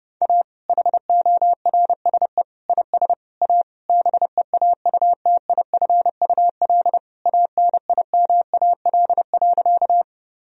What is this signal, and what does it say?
Morse code, heard as a keyed tone, about 32 wpm, 700 Hz. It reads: A HORSE IS A BEAUTIFUL ANIMAL.
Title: Sentences from Top 1000 Words - Review Part 1 - 30wpm